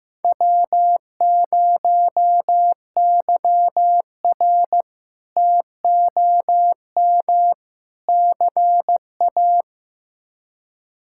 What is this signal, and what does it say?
Morse code, heard as a keyed tone, about 15 wpm, 700 Hz. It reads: W0YR TOM CA
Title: K1USN SST QSO Practice - 15wpm